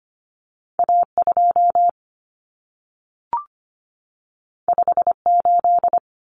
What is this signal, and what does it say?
Morse code, heard as a keyed tone, about 25 wpm, 700 Hz. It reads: A2 58
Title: Common Cut Numbers - 2-Digit 25wpm